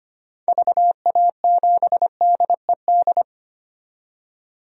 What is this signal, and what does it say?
Morse code, heard as a keyed tone, about 25 wpm, 700 Hz. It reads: VA7DED
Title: DXCC-100 Call Signs - 6 Characters - 25wpm